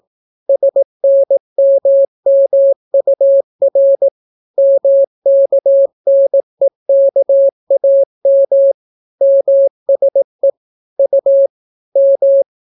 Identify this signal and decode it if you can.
Morse code, heard as a keyed tone, about 18 wpm, 550 Hz. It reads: SNMMUR MKNEKAM MSE U M